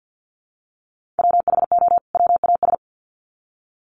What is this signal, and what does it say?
Morse code, heard as a keyed tone, about 48 wpm, 700 Hz. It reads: WHO WAS
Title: Sets of 2 Words 50wpm